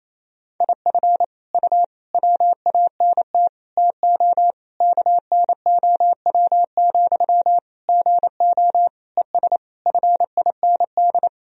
Morse code, heard as a keyed tone, about 28 wpm, 700 Hz. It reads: IF U WANT TO KNOW, GO ES FIND
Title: Sentences from Top 200 Words and Abbreviations 28wpm